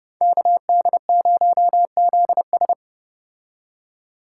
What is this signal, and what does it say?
Morse code, heard as a keyed tone, about 29 wpm, 700 Hz. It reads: KD0ZS